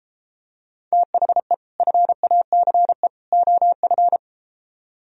Morse code, heard as a keyed tone, about 33 wpm, 700 Hz. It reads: THE FACE OF